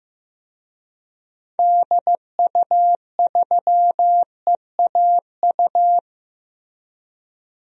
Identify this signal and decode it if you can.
Morse code, heard as a keyed tone, about 16 wpm, 700 Hz. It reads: DU3EAU
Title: DXCC-100 Call Signs - 6 Characters - Repeated 2x - 15wpm